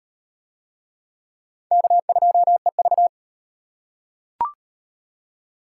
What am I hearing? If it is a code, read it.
Morse code, heard as a keyed tone, about 38 wpm, 700 Hz. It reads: K2EV